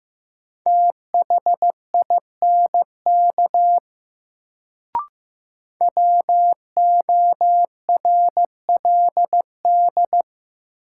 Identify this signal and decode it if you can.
Morse code, heard as a keyed tone, about 15 wpm, 700 Hz. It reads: THINK WORLD
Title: Sentences from Top 200 Words - New Word Drill - 15wpm